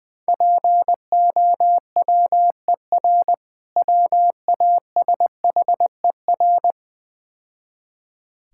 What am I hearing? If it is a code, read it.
Morse code, heard as a keyed tone, about 20 wpm, 690 Hz. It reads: POWER WASHER
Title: Difficult Words - Speed-Racing with Hints - 20wpm